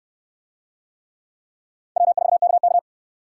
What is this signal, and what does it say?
Morse code, heard as a keyed tone, about 68 wpm, 700 Hz. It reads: W4KK